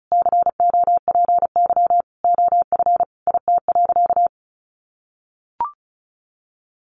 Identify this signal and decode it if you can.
Morse code, heard as a keyed tone, about 34 wpm, 700 Hz. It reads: COPY OF IT.
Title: Sentences from Top 1400 Words 35wpm